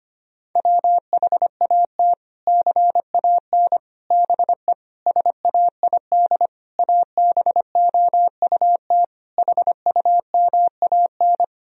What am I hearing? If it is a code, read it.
Morse code, heard as a keyed tone, about 25 wpm, 700 Hz. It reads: WHAT CAN BE SAID ABOUT HUMAN
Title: Sentences from Top 1000 Words - Review Part 4 - 25wpm